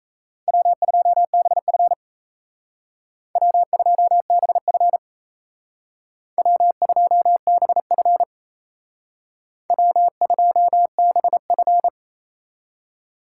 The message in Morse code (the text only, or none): W2BF W2BF W2BF W2BF